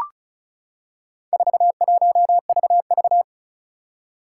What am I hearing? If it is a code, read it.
Morse code, heard as a keyed tone, about 36 wpm, 700 Hz. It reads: 41VV
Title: Four Challenging Letter-Number Combinations 35wpm